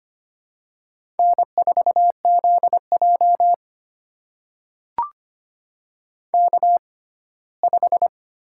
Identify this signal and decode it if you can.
Morse code, heard as a keyed tone, about 25 wpm, spaced as for 17 wpm, 700 Hz. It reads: N4ZJ K5
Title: USA Call Signs - 4 Characters - 6x Character Spacing - 25wpm